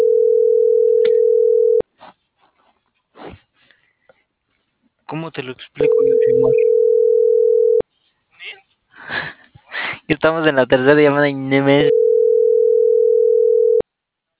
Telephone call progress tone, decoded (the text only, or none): ringback tone